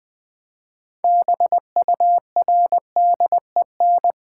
Morse code, heard as a keyed tone, about 20 wpm, 700 Hz. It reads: BURDEN
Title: Build A Word - 2 Letters At A Time - 6-Letter Words - 20wpm